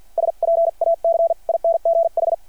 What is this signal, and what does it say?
Morse code, heard as a keyed tone, about 51 wpm, spaced as for 42 wpm, 660 Hz. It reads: SPACINGH
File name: archived / CRSnet